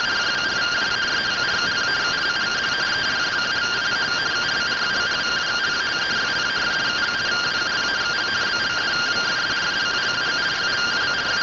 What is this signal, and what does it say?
Two-tone teleprinter signal, about 170 Hz shift.
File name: findings / MR AXIAL 71mm 3mm.gen-09 rtty rx